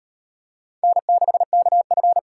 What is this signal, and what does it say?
Morse code, heard as a keyed tone, about 38 wpm, 700 Hz. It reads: N6KF